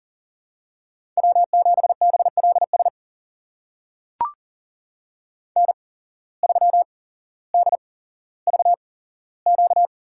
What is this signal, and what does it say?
Morse code, heard as a keyed tone, about 40 wpm, spaced as for 17 wpm, 700 Hz. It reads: W7BLS N3DVQ
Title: USA Call Signs - 5 Characters - 8x Character Spacing - 40wpm